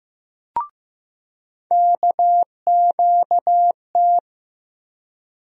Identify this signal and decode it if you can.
Morse code, heard as a keyed tone, about 15 wpm, 700 Hz. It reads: KQT
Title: USA Call Sign Postfixes - 3 Letters 15wpm